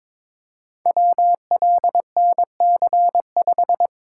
Morse code, heard as a keyed tone, about 23 wpm, 700 Hz. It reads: WLNC5
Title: Code Groups - 5 Letter:Numbers - 22wpm